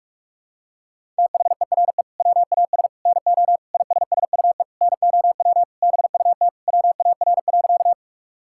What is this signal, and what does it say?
Morse code, heard as a keyed tone, about 46 wpm, 700 Hz. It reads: THERE WAS NO ISSUE NOW BUT WAR.
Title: Sentences from Top 1000 Words - Review Part 6 - 45wpm